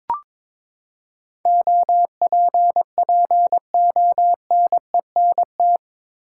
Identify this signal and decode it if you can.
Morse code, heard as a keyed tone, about 22 wpm, 700 Hz. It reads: OPPONENT
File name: practice / 22wpm / Eight Letter Words - 22wpm